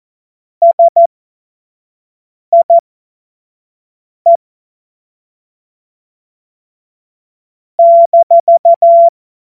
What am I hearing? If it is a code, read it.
Morse code, heard as a keyed tone, about 14 wpm, spaced as for 5 wpm, 680 Hz. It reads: SIE -
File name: Morsekurs / Witze / Witz 11. Januar 2022, 25 F70